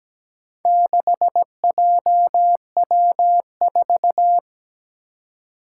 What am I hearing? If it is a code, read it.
Morse code, heard as a keyed tone, about 17 wpm, 700 Hz. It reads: 6JW4
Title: Four Challenging Letter-Number Combinations 17wpm